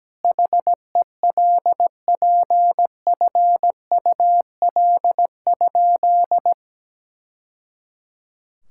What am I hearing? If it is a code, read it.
Morse code, heard as a keyed tone, about 17 wpm, 700 Hz. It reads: HELPFUL?
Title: Sentences from Top 1000 Words - Review Part 6 - 17wpm